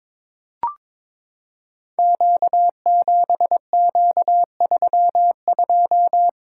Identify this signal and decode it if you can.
Morse code, heard as a keyed tone, about 22 wpm, 700 Hz. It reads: Q7Q32
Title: Code Groups - 5 Letter:Numbers - 22wpm